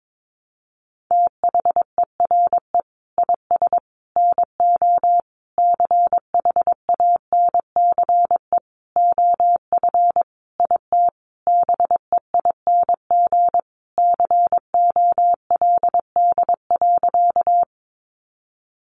Morse code, heard as a keyed tone, about 22 wpm, 700 Hz. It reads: THERE IS NO CHANCE OF IT BEING COLD.
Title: Sentences from Top 800 Words 22wpm